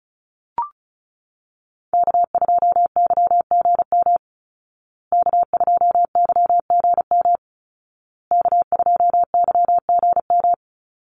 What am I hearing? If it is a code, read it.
Morse code, heard as a keyed tone, about 34 wpm, 700 Hz. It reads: K2YGM K2YGM K2YGM
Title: Call Signs Repeated 2x - 35wpm